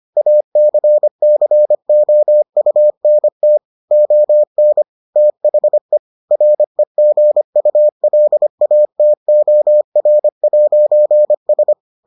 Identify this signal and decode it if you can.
Morse code, heard as a keyed tone, about 25 wpm, 600 Hz. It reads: ACCOUNT ON THE REGULATOR'S